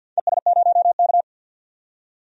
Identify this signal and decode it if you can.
Morse code, heard as a keyed tone, about 52 wpm, 700 Hz. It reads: ES0K
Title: DXCC-100 Call Signs - 4 Characters - 50wpm